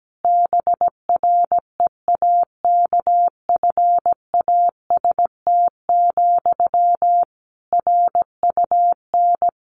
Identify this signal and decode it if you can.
Morse code, heard as a keyed tone, about 17 wpm, 700 Hz. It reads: BREAKFAST, RUN
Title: Sentences from Top 1500 Words 17wpm